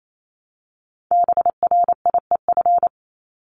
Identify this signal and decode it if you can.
Morse code, heard as a keyed tone, about 28 wpm, 700 Hz. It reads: BRIEF
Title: LY Suffix - Top 100 Words with Base Words 28wpm